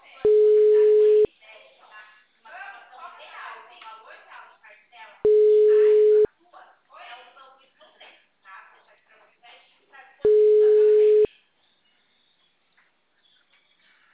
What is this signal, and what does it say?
Ringback tone (Europe, 425 Hz), with speech.
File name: out- -2004- - - .2648